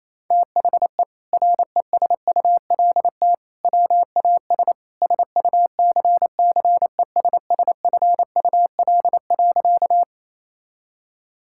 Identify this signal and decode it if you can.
Morse code, heard as a keyed tone, about 27 wpm, 700 Hz. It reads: THE RESULT WAS SUCCESSFUL.
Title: Sentences from Top 1000 Words - Review Part 6 - 28wpm